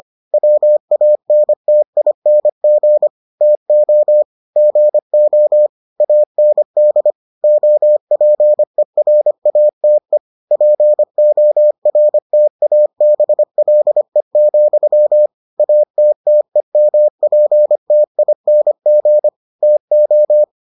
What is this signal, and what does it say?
Morse code, heard as a keyed tone, about 25 wpm, 600 Hz. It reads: WANTING TO GO AND OPERATE PORTABLE, ATTEMPTING TO